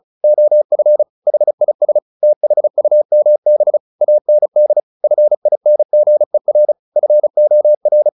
Morse code, heard as a keyed tone, about 34 wpm, 600 Hz. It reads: OF HIS THUMB AND FINGER FOR
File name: the old man and the sea / the old man and the sea 1 35wpm